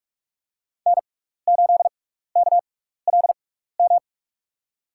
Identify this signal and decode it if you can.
Morse code, heard as a keyed tone, about 46 wpm, spaced as for 16 wpm, 700 Hz. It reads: N8KLM
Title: USA Call Signs - 5 Characters - 6x Character Spacing - 45wpm